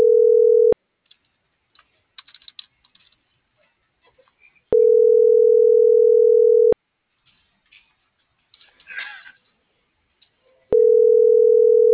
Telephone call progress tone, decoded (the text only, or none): ringback tone